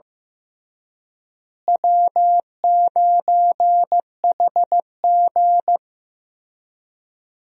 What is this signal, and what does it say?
Morse code, heard as a keyed tone, about 15 wpm, 700 Hz. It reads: W9HG
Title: USA Call Signs - 4 Characters - Repeated 3x - 15wpm